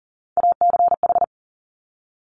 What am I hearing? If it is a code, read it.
Morse code, heard as a keyed tone, about 42 wpm, 700 Hz. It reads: ACH
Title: Top Three Letter Combinations in Words 40wpm